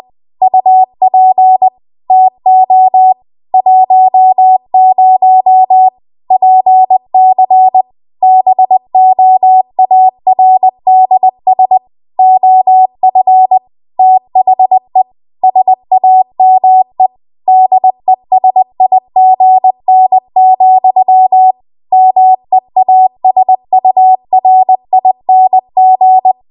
Morse code, heard as a keyed tone, about 20 wpm, 750 Hz. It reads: UP TO 10 PC BOARDS OF THE SAME DESIGN, MEASURING
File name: Archive / 20 WPM / 20WPM